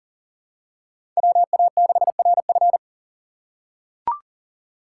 Morse code, heard as a keyed tone, about 37 wpm, 700 Hz. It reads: WA6RF
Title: USA Call Signs - 5 Characters - 4x Character Spacing - 40wpm